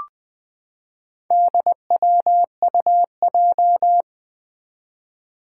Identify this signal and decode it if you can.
Morse code, heard as a keyed tone, about 20 wpm, 700 Hz. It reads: DWUJ
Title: Four Challenging Letter-Number Combinations 20wpm